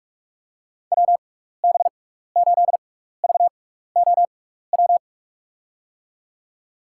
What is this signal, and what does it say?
Morse code, heard as a keyed tone, about 41 wpm, spaced as for 15 wpm, 700 Hz. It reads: WB8VOW
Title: USA Call Signs - 6 Characters - 6x Character Spacing - 45wpm